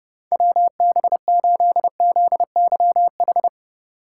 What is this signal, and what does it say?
Morse code, heard as a keyed tone, about 32 wpm, 700 Hz. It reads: WB8ZYH
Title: USA Call Signs - 8x Character Spacing - 30wpm